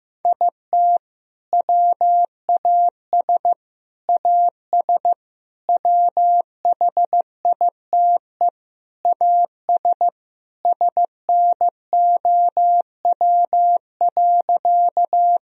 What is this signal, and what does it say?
Morse code, heard as a keyed tone, about 15 wpm, 700 Hz. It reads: IT WAS AS WHITE AS SNOW.
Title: Sentences from Top 1000 Words - Review Part 3 - 15wpm